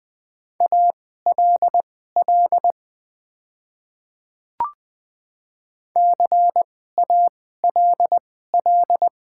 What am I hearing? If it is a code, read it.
Morse code, heard as a keyed tone, about 20 wpm, spaced as for 14 wpm, 700 Hz. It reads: ALL CALL
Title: 2x Character Spacing - Top 100 Words - 20wpm